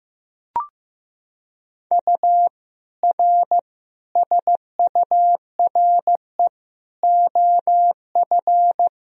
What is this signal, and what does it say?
Morse code, heard as a keyed tone, about 15 wpm, 700 Hz. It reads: U R SURE OF
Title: Sentences from Top 300 Words and Abbreviations 15wpm